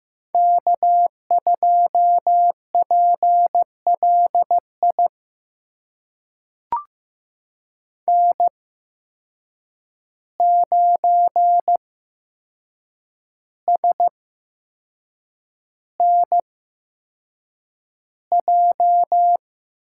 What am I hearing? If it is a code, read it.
Morse code, heard as a keyed tone, about 15 wpm, spaced as for 6 wpm, 700 Hz. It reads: K2PLI N9SNJ
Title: USA Call Signs - 5 Characters - 8x Character Spacing - 15wpm